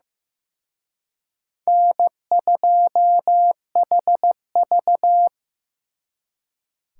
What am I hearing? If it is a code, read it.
Morse code, heard as a keyed tone, about 18 wpm, 700 Hz. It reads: N2HV
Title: Speed Racing - USA 1x2 Call Signs 15wpm